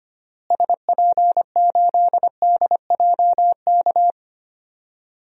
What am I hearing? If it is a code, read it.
Morse code, heard as a keyed tone, about 25 wpm, 700 Hz. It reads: SP8DJK
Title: DXCC-100 Call Signs - 6 Characters - Repeated 3x - 25wpm